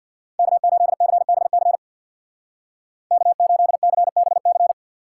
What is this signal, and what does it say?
Morse code, heard as a keyed tone, about 61 wpm, 700 Hz. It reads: K8CBC K8CBC
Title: Speed Racing - USA Call Signs 50wpm